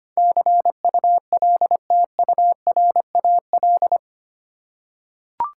Morse code, heard as a keyed tone, about 25 wpm, 700 Hz. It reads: CULTURAL
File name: 8 Letter Words - First 3 Letters Given - 25wpm